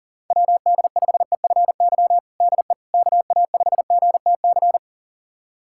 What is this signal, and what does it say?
Morse code, heard as a keyed tone, about 40 wpm, 700 Hz. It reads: WD5EFY DE KA5GTC
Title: USA Callsign DE Callsign - 6 Characters - 40wpm